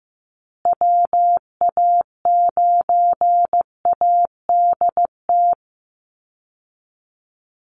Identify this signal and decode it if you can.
Morse code, heard as a keyed tone, about 15 wpm, 700 Hz. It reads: WA9ADT
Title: Call Signs Repeated 3x - 15wpm